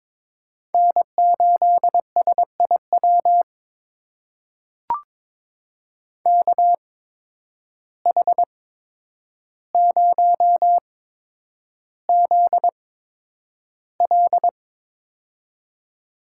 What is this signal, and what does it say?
Morse code, heard as a keyed tone, about 22 wpm, spaced as for 9 wpm, 700 Hz. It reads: N8SIW KH0ZL
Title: USA Call Signs - 5 Characters - 8x Character Spacing - 22wpm